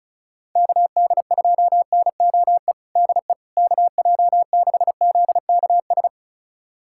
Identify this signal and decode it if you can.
Morse code, heard as a keyed tone, about 36 wpm, 700 Hz. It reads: KD2NOE DE KJ6ZKS
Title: USA Callsign DE Callsign - 6 Characters - 35wpm